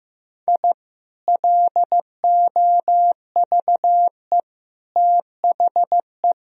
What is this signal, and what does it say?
Morse code, heard as a keyed tone, about 15 wpm, 700 Hz. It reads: I LOVE THE
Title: Sentences from Top 1000 Words - Review Part 3 - 15wpm